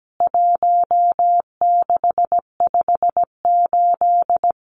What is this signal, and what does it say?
Morse code, heard as a keyed tone, about 17 wpm, 700 Hz. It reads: 1658
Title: CWT Exchange 17wpm